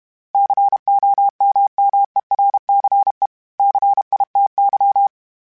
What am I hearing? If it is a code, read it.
Morse code, heard as a keyed tone, about 32 wpm, 800 Hz. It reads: COMMERCE CITY
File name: fromSMs / FromSMs mp3s / Cities@30